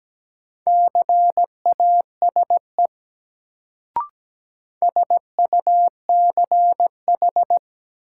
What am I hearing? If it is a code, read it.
Morse code, heard as a keyed tone, about 17 wpm, 700 Hz. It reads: CASE SUCH
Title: Sets of 2 Words - New Word Drill - 17wpm